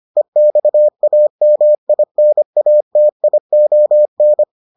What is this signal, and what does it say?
Morse code, heard as a keyed tone, about 25 wpm, 600 Hz. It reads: EXAMINATION